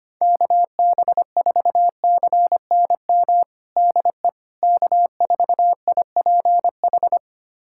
Morse code, heard as a keyed tone, about 25 wpm, 700 Hz. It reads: KB4CNM DE K4IPH
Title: USA Callsign DE Callsign 25wpm